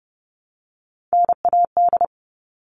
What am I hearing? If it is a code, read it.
Morse code, heard as a keyed tone, about 30 wpm, 700 Hz. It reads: NAD
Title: All Cut Numbers - 3-Digit 30wpm